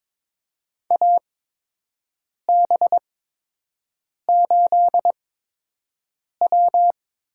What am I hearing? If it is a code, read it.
Morse code, heard as a keyed tone, about 21 wpm, spaced as for 6 wpm, 700 Hz. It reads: AB8W
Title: USA Call Signs - 8x Character Spacing - 22wpm